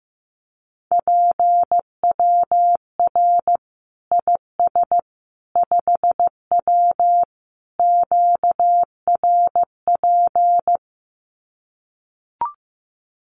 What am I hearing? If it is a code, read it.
Morse code, heard as a keyed tone, about 15 wpm, 700 Hz. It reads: PWR IS 5W QRP
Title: Common QSO Phrases 15wpm